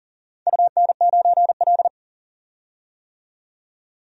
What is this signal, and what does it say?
Morse code, heard as a keyed tone, about 39 wpm, 700 Hz. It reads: UN9L
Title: DXCC-100 Call Signs - 4 Characters - 40wpm